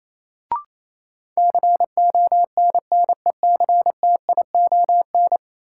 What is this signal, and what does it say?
Morse code, heard as a keyed tone, about 28 wpm, 700 Hz. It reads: CONNECTION
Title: Sentences from Top 1300 Words - New Word Drill Alt - 28wpm